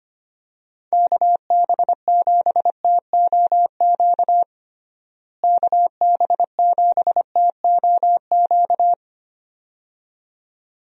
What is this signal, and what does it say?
Morse code, heard as a keyed tone, about 25 wpm, 700 Hz. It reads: KB7TOQ KB7TOQ
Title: USA Call Signs - 6 Characters - Repeated 2x - 25wpm